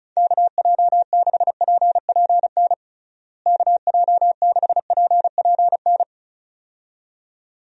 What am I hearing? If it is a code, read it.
Morse code, heard as a keyed tone, about 34 wpm, 700 Hz. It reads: KJ6PPN KJ6PPN
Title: USA Call Signs - 6 Characters - Repeated 2x - 35wpm